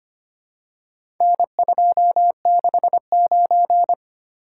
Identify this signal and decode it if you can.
Morse code, heard as a keyed tone, about 25 wpm, 700 Hz. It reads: N269
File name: practice / 25wpm / Four Letter-Number Combinations - 25wpm